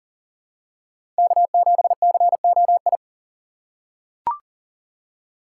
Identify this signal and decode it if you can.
Morse code, heard as a keyed tone, about 40 wpm, 700 Hz. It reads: K7COI
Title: USA Call Signs - 5 Characters - 2x Character Spacing - 40wpm